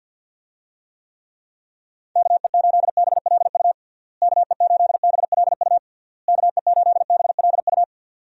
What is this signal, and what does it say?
Morse code, heard as a keyed tone, about 50 wpm, 700 Hz. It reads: KE8BLU KE8BLU KE8BLU
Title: USA Call Signs - 6 Characters - Repeated 3x - 50wpm